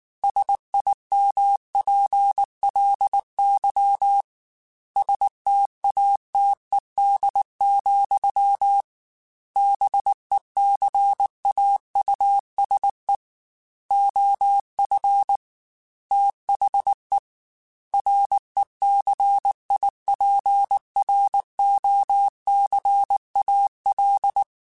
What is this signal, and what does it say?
Morse code, heard as a keyed tone, about 19 wpm, 800 Hz. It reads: SIMPLY STATED, BECAUSE OF THE RECIPROCAL
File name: Technical Subjects / Technical Subjects-mp3s / XC @18